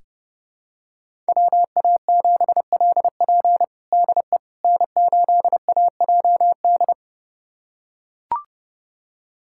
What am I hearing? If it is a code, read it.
Morse code, heard as a keyed tone, about 32 wpm, 700 Hz. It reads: WA7LP DE N8AJD
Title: USA Callsign DE Callsign - 5 Characters - 30wpm